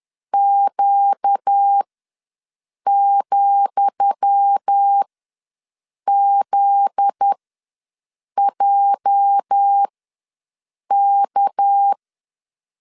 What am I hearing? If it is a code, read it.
Morse code, heard as a keyed tone, about 10 wpm, spaced as for 6 wpm, 800 Hz. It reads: Q,ZJK
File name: LEZIONI / 10 SIGNS / 10D COMA ws 10wpm 4314